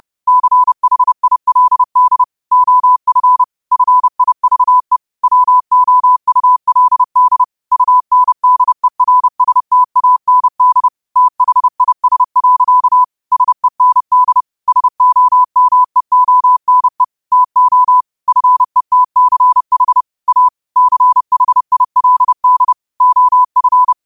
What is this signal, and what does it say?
Morse code, heard as a keyed tone, about 30 wpm, 1000 Hz. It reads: CHILD OF FIVE WOULD UNDERSTAND THIS. SEND SOMEONE TO FETCH A CHILD OF
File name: qotdfiles / 30wpm